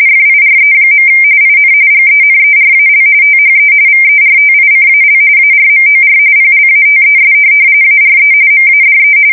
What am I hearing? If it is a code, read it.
RTTY, about 45.45 baud. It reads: FIRE. IN GENERAL THE VERTICAL ARMOR PLATES ARE MADE
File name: fsk170 / USS-Missouri